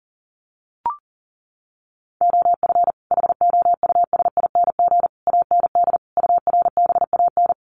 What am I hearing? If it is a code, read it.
Morse code, heard as a keyed tone, about 40 wpm, 700 Hz. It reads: OF HOUSING AND URBAN
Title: Sets of 5 Words Encore2 40wpm